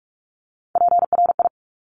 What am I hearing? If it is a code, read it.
Morse code, heard as a keyed tone, about 46 wpm, 700 Hz. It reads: PRI
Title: Top Three Letter Combinations in Words 45wpm